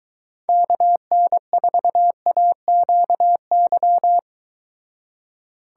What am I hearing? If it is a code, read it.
Morse code, heard as a keyed tone, about 23 wpm, 700 Hz. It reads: KN4AQY